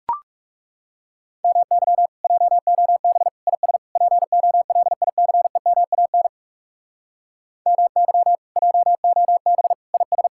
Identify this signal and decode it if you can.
Morse code, heard as a keyed tone, about 38 wpm, 700 Hz. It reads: MY JOB IS POLICEMAN MY JOB IS